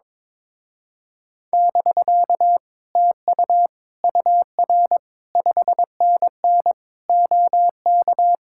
Morse code, heard as a keyed tone, about 22 wpm, 700 Hz. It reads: <BK> TU UR 5NN OK
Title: Speed-Racing Common QSO Phrases - 22wpm